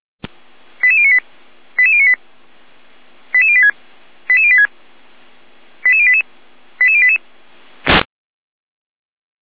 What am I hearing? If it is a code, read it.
Five-tone selective call (ZVEI), heard as quick runs of five tones, tones of about 70 ms. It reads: 88098 88098 88086 88086 88088 88088